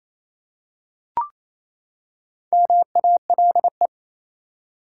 Morse code, heard as a keyed tone, about 28 wpm, 700 Hz. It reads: MALE
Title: Four Letter Words - 28wpm